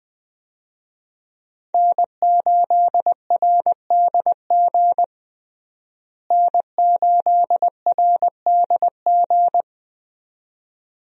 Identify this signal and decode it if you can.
Morse code, heard as a keyed tone, about 20 wpm, 700 Hz. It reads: N8RDG N8RDG
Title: USA Call Signs - 5 Characters - Repeated 2x - 20wpm